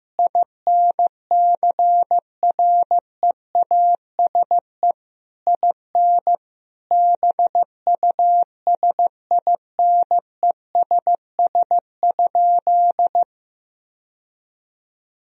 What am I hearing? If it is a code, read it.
Morse code, heard as a keyed tone, about 15 wpm, 700 Hz. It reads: INCREASE IN BUSINESS?
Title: Sentences from Top 1000 Words - Review Part 4 - 15wpm